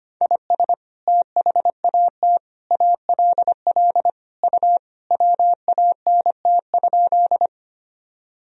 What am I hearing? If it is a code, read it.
Morse code, heard as a keyed tone, about 25 wpm, 700 Hz. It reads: IS THAT ALL U WANT?
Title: Sentences from Top 200 Words and Abbreviations 25wpm